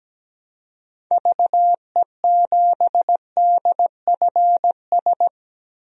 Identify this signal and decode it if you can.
Morse code, heard as a keyed tone, about 17 wpm, 700 Hz. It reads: VE7DFS
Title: DXCC-100 Call Signs - 6 Characters - Repeated 3x - 17wpm